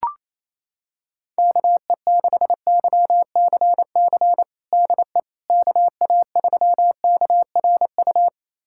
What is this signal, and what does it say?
Morse code, heard as a keyed tone, about 28 wpm, 700 Hz. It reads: KE6YCC DE KA3KRU
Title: USA Callsign DE Callsign 28wpm